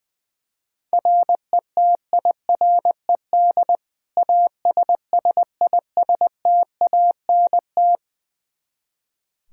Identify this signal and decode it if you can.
Morse code, heard as a keyed tone, about 20 wpm, 700 Hz. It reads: RETIRED ASSISTANT
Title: Practice QSO - Exchange 3 - More Info - 20wpm